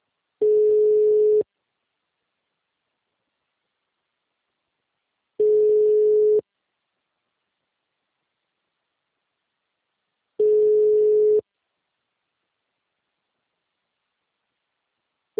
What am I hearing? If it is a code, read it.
Ringback tone (Europe, 425 Hz).
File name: out- -2010- - - .234